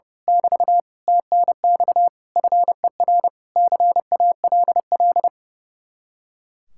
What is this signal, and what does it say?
Morse code, heard as a keyed tone, about 28 wpm, 700 Hz. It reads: <BT> TNX FER CALL
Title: Speed-Racing Common QSO Phrases Encore - 30wpm